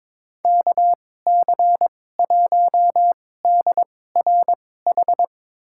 Morse code, heard as a keyed tone, about 22 wpm, spaced as for 16 wpm, 700 Hz. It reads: KC1DRH
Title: USA Call Signs - 6 Characters - 2x Character Spacing - 22wpm